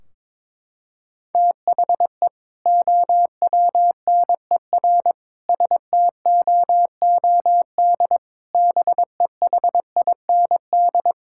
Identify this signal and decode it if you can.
Morse code, heard as a keyed tone, about 22 wpm, 700 Hz. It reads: THE OWNER STOOD BEHIND
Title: Sentences from Top 1600 Words 22wpm